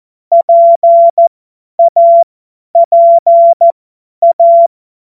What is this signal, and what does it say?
Morse code, heard as a keyed tone, about 14 wpm, spaced as for 10 wpm, 680 Hz. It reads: PAPA
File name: Morsekurs / Witze / Witz 04. Oktober 2022 50 F70